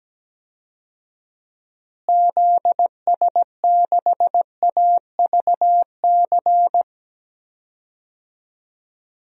Morse code, heard as a keyed tone, about 17 wpm, 700 Hz. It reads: ZS6AVC
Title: DXCC-100 Call Signs - 6 Characters - Repeated 3x - 17wpm